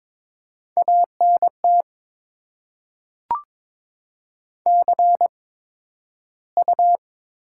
Morse code, heard as a keyed tone, about 22 wpm, spaced as for 12 wpm, 700 Hz. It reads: ANT CU
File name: Top 20 QSO Elements - 8x Character Spacing - 22wpm